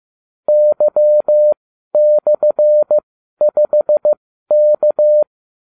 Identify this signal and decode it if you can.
Morse code, heard as a keyed tone, about 15 wpm, spaced as for 12 wpm, 600 Hz. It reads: Y/5K